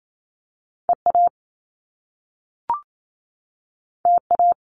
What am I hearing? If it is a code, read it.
Morse code, heard as a keyed tone, about 28 wpm, 700 Hz. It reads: EA TA
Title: Top Two Letter Combinations in Words 28wpm